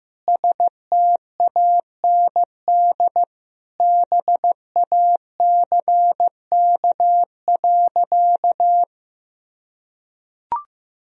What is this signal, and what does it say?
Morse code, heard as a keyed tone, about 15 wpm, 700 Hz. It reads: STAND BACK.
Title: Sentences from Top 1000 Words - Review Part 1 - 15wpm